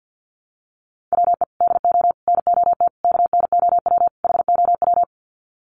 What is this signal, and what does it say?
Morse code, heard as a keyed tone, about 50 wpm, 700 Hz. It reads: WE DO NOT KNOW HOW
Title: Sets of 5 Words Encore 50wpm